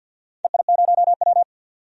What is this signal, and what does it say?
Morse code, heard as a keyed tone, about 51 wpm, 700 Hz. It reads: EI0W
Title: DXCC-100 Call Signs - 4 Characters - 50wpm